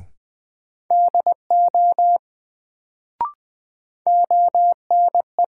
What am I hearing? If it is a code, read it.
Morse code, heard as a keyed tone, about 20 wpm, 700 Hz. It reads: DO ONE
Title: Sets of 2 Words - New Word Drill - 20wpm